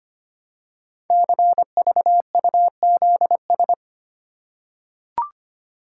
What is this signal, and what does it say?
Morse code, heard as a keyed tone, about 25 wpm, 690 Hz. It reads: CVUZS
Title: Code Groups - 5 Letters - 25wpm